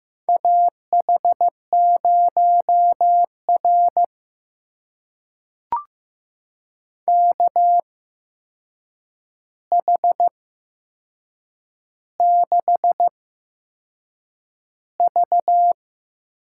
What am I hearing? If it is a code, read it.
Morse code, heard as a keyed tone, about 15 wpm, spaced as for 6 wpm, 700 Hz. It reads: AH0R KH6V
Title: USA Call Signs - 8x Character Spacing - 15wpm